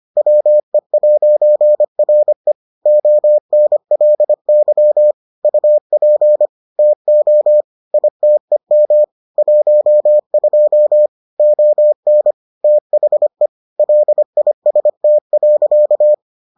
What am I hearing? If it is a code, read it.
Morse code, heard as a keyed tone, about 25 wpm, 600 Hz. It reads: WE'RE ONLY UP TO ITEM 12 ON THE LIST.